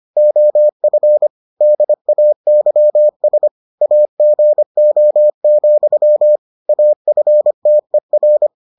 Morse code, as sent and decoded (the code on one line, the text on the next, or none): --- ..-. / -.. .- -.-- ... / .- --. --- --..-- / .- ..-. - . .-.
OF DAYS AGO, AFTER